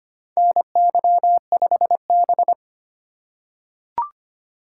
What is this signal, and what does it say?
Morse code, heard as a keyed tone, about 25 wpm, 700 Hz. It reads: NY5B
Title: USA Call Signs - 4 Characters - 3x Character Spacing - 25wpm